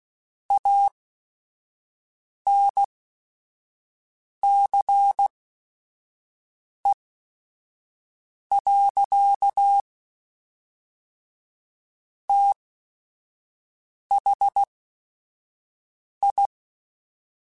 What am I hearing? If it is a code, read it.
Morse code, heard as a keyed tone, about 16 wpm, spaced as for 5 wpm, 800 Hz. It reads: ANCE. THI
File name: Technical Subjects / Technical Subjects-mp3s / XC @5